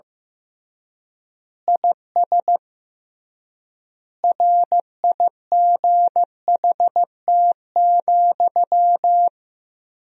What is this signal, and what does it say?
Morse code, heard as a keyed tone, about 15 wpm, 700 Hz. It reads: IS RIGHT,
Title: 3x Word-Spacing - Sentences from Top 200 Words - 15wpm